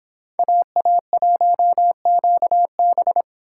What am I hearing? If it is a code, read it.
Morse code, heard as a keyed tone, about 26 wpm, 700 Hz. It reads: AA1QB